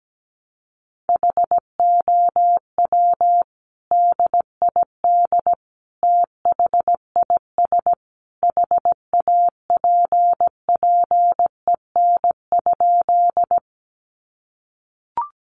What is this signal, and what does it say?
Morse code, heard as a keyed tone, about 17 wpm, 700 Hz. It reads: HOW DID THIS HAPPEN?
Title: Sentences from Top 400 Words 17wpm